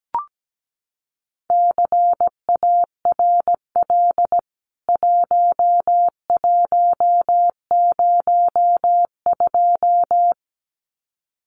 Morse code, heard as a keyed tone, about 17 wpm, 700 Hz. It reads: CARL 1102
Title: CWT Exchange 17wpm